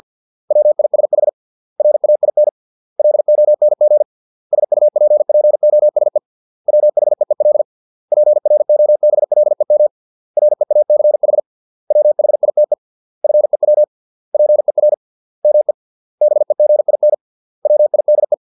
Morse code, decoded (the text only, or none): WISH RAIN LONG SUPPOSE WHEEL PROBLEM REACH WHITE FEW PER ME BEGIN WIDE